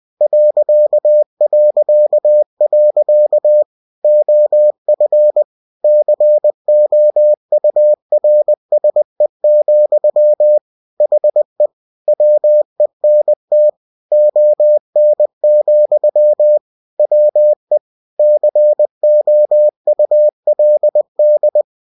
Morse code, heard as a keyed tone, about 20 wpm, 600 Hz. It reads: ... OF COURSE, HE WENT ON, WE COULD
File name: cw-books / winnie-the-pooh / winnie-20wpm-page-0061